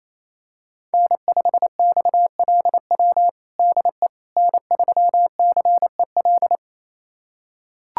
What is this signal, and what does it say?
Morse code, heard as a keyed tone, about 27 wpm, 700 Hz. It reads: N5XLW DE N3CEL
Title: USA Callsign DE Callsign 28wpm